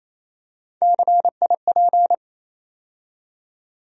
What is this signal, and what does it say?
Morse code, heard as a keyed tone, about 28 wpm, 700 Hz. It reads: CIP
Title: USA Call Sign Postfixes - 3 Letters 28wpm